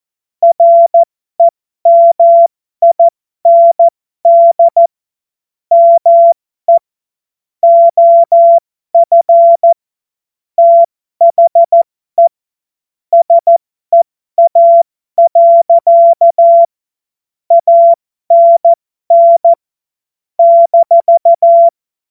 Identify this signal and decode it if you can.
Morse code, heard as a keyed tone, about 14 wpm, spaced as for 12 wpm, 680 Hz. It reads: REMIND ME OF THE SEA. ANN -
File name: Morsekurs / Witze